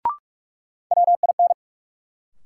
Morse code, heard as a keyed tone, about 45 wpm, 700 Hz. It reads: WIN
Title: Sentences from Top 1500 Words - New Word Drill - 45wpm